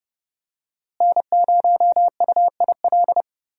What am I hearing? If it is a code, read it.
Morse code, heard as a keyed tone, about 30 wpm, 700 Hz. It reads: N0UIL
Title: USA Call Signs - 5 Characters - Repeated 2x - 30wpm